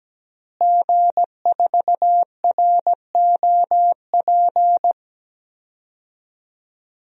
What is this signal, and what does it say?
Morse code, heard as a keyed tone, about 17 wpm, 700 Hz. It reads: G4ROP
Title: DXCC-100 Call Signs - 5 Characters - Repeated 2x - 17wpm